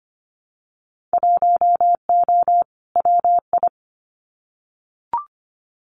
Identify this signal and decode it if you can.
Morse code, heard as a keyed tone, about 25 wpm, 700 Hz. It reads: 1O WI
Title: Winter Field Day Exchanges Encore 25wpm